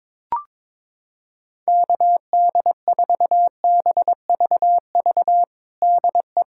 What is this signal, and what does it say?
Morse code, heard as a keyed tone, about 21 wpm, 700 Hz. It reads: KD4BVV DE
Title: USA Callsign DE Callsign - 6 Characters - 22wpm